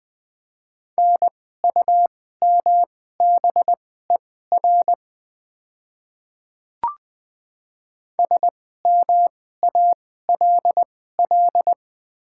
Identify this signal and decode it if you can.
Morse code, heard as a keyed tone, about 20 wpm, spaced as for 14 wpm, 700 Hz. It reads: NUMBER SMALL
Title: 2x Character Spacing - Top 200 Words - 20wpm